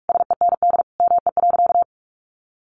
Morse code, heard as a keyed tone, about 44 wpm, 700 Hz. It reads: SEND ME.
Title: Sentences from Top 1500 Words 45wpm